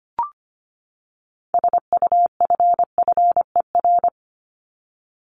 Morse code, heard as a keyed tone, about 25 wpm, 700 Hz. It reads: SUFFER
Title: Top 1501-1750 Words 25wpm